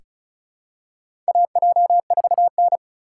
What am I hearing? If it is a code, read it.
Morse code, heard as a keyed tone, about 34 wpm, 700 Hz. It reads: AJ4N
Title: USA Call Signs - 4 Characters - 3x Character Spacing - 35wpm